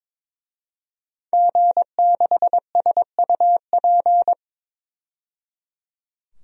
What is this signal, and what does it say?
Morse code, heard as a keyed tone, about 22 wpm, 700 Hz. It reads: G6SUP
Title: DXCC-100 Call Signs - 5 Characters - 22wpm